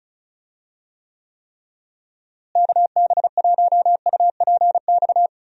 Morse code, heard as a keyed tone, about 36 wpm, 700 Hz. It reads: KB1UPX
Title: USA Call Signs - 6 Characters - Repeated 3x - 35wpm